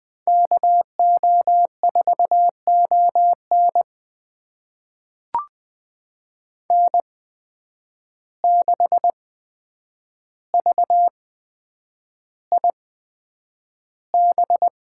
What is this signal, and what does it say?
Morse code, heard as a keyed tone, about 20 wpm, spaced as for 9 wpm, 700 Hz. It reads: KO4ON N6VIB
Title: USA Call Signs - 5 Characters - 8x Character Spacing - 20wpm